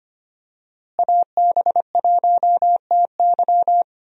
Morse code, heard as a keyed tone, about 25 wpm, 700 Hz. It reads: AB1TY
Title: USA Call Signs - 5 Characters - Repeated 2x - 25wpm